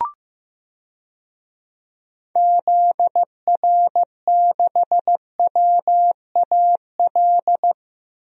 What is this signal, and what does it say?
Morse code, heard as a keyed tone, about 15 wpm, 700 Hz. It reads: ZR6WAL
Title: DXCC-100 Call Signs - 6 Characters - Repeated 3x - 15wpm